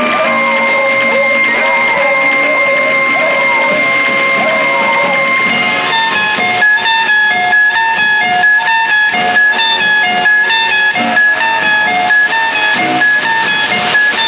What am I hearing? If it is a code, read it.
Dial-up modem handshake.
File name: wdog-pirate-radio-broadcast-featuring-1954-and-1964 / WDOG-Long